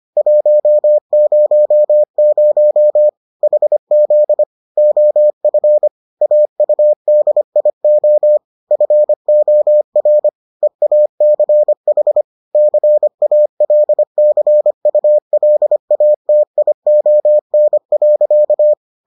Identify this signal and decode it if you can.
Morse code, heard as a keyed tone, about 25 wpm, 600 Hz. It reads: 100 HZ OF AUDIO FOR EACH CALCULATION.